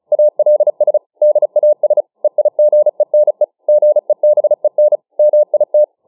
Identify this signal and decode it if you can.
Morse code, heard as a keyed tone, about 36 wpm, 600 Hz. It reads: ALS DAS EIGENE GEBEN MIT